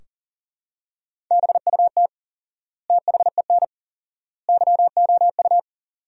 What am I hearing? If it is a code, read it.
Morse code, heard as a keyed tone, about 40 wpm, 700 Hz. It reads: BUT THEN YOU
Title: 4x Word-Spacing - Sentences from Top 300 Words - 40wpm